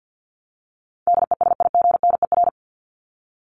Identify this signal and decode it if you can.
Morse code, heard as a keyed tone, about 50 wpm, 700 Hz. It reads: DESIGNER
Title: Top 1501-1750 Words 50wpm